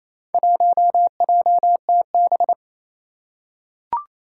Morse code, heard as a keyed tone, about 28 wpm, 700 Hz. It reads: 1JTB
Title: Four Challenging Letter-Number Combinations 28wpm